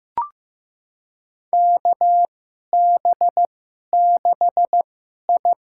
Morse code, heard as a keyed tone, about 15 wpm, spaced as for 11 wpm, 700 Hz. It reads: KB6I
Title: USA Call Signs - 2x Character Spacing - 15wpm